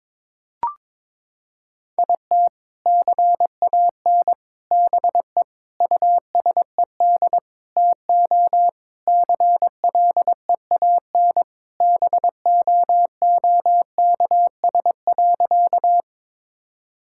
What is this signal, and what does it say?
Morse code, heard as a keyed tone, about 22 wpm, 700 Hz. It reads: IT CAN BE USED TO CLEAN BOOKS.
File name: practice / 22wpm / Sentences with 4-Word Phrases - 22wpm